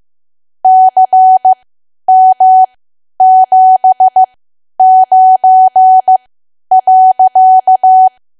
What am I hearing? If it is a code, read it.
Morse code, heard as a keyed tone, about 15 wpm, spaced as for 10 wpm, 750 Hz. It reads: CM79.
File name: Archive / 10 WPM / 10WPM